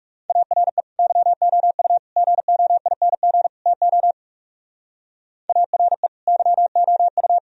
Morse code, heard as a keyed tone, about 37 wpm, 700 Hz. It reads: ARE YOU GOING TO ARE YOU
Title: Speed-Racing - Sets of 4 Words 30wpm